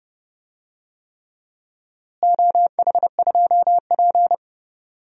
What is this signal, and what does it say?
Morse code, heard as a keyed tone, about 31 wpm, 700 Hz. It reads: OH2P
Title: DXCC-100 Call Signs - 4 Characters - Repeated 3x - 30wpm